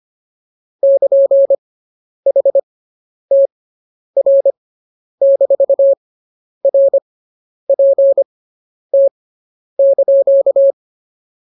Morse code, heard as a keyed tone, about 24 wpm, spaced as for 10 wpm, 550 Hz. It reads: <KN> HTR-RPT)